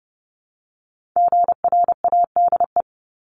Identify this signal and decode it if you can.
Morse code, heard as a keyed tone, about 30 wpm, 700 Hz. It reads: GRADE
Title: Multi-letter Phonemes 30wpm